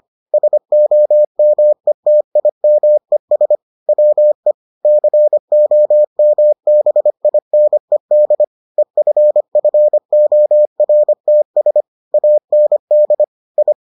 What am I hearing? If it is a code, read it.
Morse code, heard as a keyed tone, about 25 wpm, 600 Hz. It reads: SOMETIMES WE COMBINED EFFORTS AND I